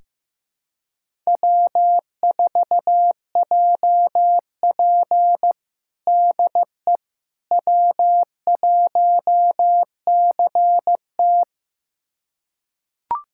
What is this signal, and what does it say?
Morse code, heard as a keyed tone, about 15 wpm, 700 Hz. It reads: W4JP DE W1CT
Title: USA Callsign DE Callsign - 4 Characters - 15wpm